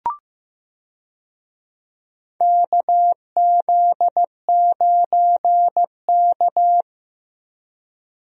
Morse code, heard as a keyed tone, about 16 wpm, 700 Hz. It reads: KZ9K
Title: USA Call Signs - 4 Characters - Repeated 3x - 15wpm